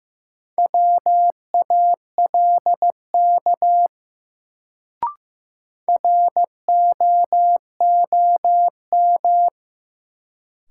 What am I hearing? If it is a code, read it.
Morse code, heard as a keyed tone, about 15 wpm, 700 Hz. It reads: WALK ROOM
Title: Sentences from Top 300 Words - New Word Drill - 15wpm